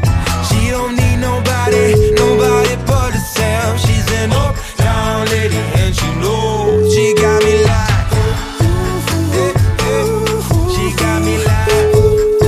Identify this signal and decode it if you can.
Ringback tone (Europe, 425 Hz), with noise.